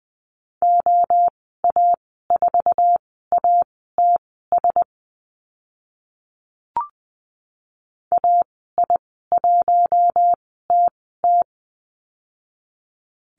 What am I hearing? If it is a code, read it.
Morse code, heard as a keyed tone, about 20 wpm, spaced as for 14 wpm, 700 Hz. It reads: OA4ATS AI1TT
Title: 062 Lesson 11g - Letter h - Call Signs 20wpm